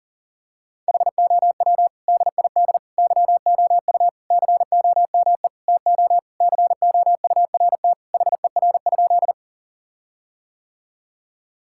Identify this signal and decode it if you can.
Morse code, heard as a keyed tone, about 40 wpm, 700 Hz. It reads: HOW DID YOU COME TO COURT HER?
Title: Sentences from Top 1600 Words 40wpm